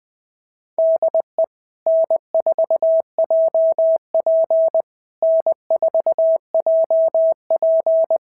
Morse code, heard as a keyed tone, about 19 wpm, 650 Hz. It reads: DE N4JP N4JP
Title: SOTA QSO Practice 20wpm